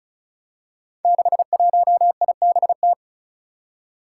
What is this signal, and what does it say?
Morse code, heard as a keyed tone, about 36 wpm, 700 Hz. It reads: 61IBT
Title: Five Challenging Letter-Number Combinations 35wpm